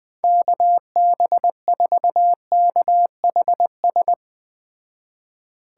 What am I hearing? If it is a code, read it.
Morse code, heard as a keyed tone, about 20 wpm, 700 Hz. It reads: KB4KHS
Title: Speed Racing - USA 2x3 Call Signs 17wpm